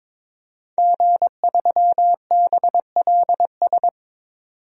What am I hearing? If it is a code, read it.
Morse code, heard as a keyed tone, about 23 wpm, 700 Hz. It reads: G3BLS
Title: DXCC-100 Call Signs - 5 Characters - 22wpm